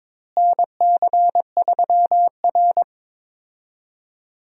Morse code, heard as a keyed tone, about 22 wpm, 700 Hz. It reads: NC3R
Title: USA Call Signs - 4 Characters - Repeated 2x - 22wpm